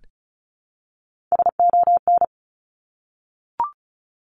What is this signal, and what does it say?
Morse code, heard as a keyed tone, about 36 wpm, 700 Hz. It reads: SON
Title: Top Three Letter Combinations in Words 35wpm